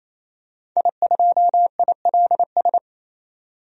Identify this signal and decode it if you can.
Morse code, heard as a keyed tone, about 28 wpm, 700 Hz. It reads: I2ILS
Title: DXCC-100 Call Signs - 5 Characters - 28wpm